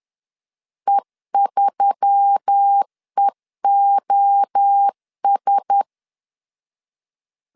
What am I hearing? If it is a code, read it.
Morse code, heard as a keyed tone, about 10 wpm, 800 Hz. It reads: E3EOS
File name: CW / DET / 1 6 1500C S B 10wpm 4137